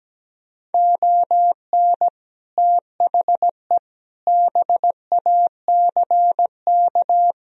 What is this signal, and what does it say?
Morse code, heard as a keyed tone, about 17 wpm, 700 Hz. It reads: ON THE BACK
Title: Speed-Racing - Sets of 5 Words 17wpm